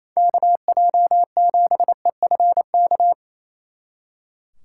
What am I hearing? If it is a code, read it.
Morse code, heard as a keyed tone, about 34 wpm, 700 Hz. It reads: KJ7EFK
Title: Speed Racing - USA 2x3 Call Signs 28wpm